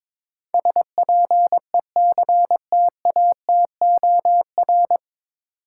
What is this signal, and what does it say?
Morse code, heard as a keyed tone, about 21 wpm, 700 Hz. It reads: SPECTATOR
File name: 9 Letter Words - First 3 Letters Given - 22wpm